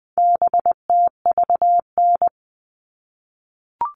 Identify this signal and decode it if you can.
Morse code, heard as a keyed tone, about 20 wpm, 700 Hz. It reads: BTVN
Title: All Cut Numbers - 4-Digit 20wpm